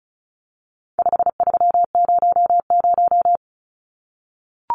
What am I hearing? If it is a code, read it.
Morse code, heard as a keyed tone, about 34 wpm, 700 Hz. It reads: 5300
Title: Numbers - 4-Digit 35wpm